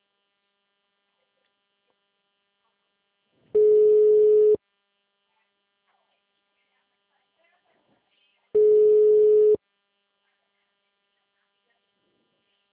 Ringback tone (Europe, 425 Hz).